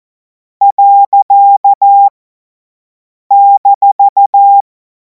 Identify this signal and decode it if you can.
Morse code, heard as a keyed tone, about 14 wpm, 800 Hz. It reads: . -
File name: Morsekurs / Witze / Witz 14. Juli 2026 50 F70